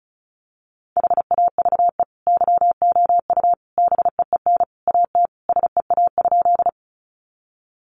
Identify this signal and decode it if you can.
Morse code, heard as a keyed tone, about 34 wpm, 700 Hz. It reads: HAVE YOU BEEN AT SEA?
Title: Sentences from Top 300 Words 35wpm